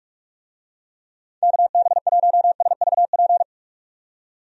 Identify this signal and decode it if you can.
Morse code, heard as a keyed tone, about 41 wpm, 700 Hz. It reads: KB1SVP
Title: USA Call Signs - 6 Characters - 45wpm